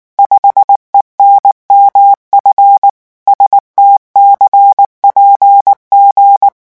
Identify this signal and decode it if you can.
Morse code, heard as a keyed tone, about 19 wpm, 800 Hz. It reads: 5ENMF STCPG
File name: morse mp3 / revision10-vitesse 18 mots